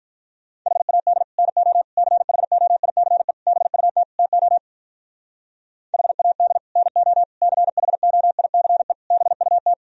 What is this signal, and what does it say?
Morse code, heard as a keyed tone, about 43 wpm, 700 Hz. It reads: HAD NO CHOICE BUT TO HAD NO CHOICE BUT
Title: Speed-Racing - Sets of 5 Words 35wpm